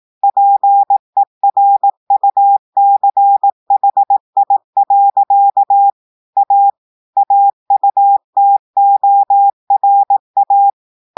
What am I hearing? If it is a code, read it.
Morse code, heard as a keyed tone, about 18 wpm, 800 Hz. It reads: PERUCHI. A AUTORA
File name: audios / portugais / portugais texte en clair 1 vitesse 18 mots minutes